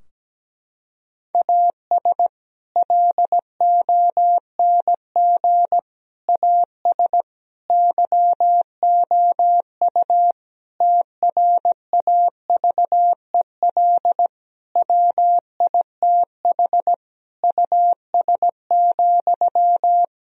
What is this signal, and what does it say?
Morse code, heard as a keyed tone, about 17 wpm, 700 Hz. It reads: AS LONG AS YOU TRAVEL WITH US,
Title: Sentences from Top 1000 Words - Review Part 2 - 17wpm